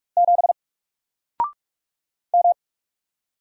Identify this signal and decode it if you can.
Morse code, heard as a keyed tone, about 44 wpm, 700 Hz. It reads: 7 M
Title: Single Character (Letter:Number:Punctuation:Prosign) 45wpm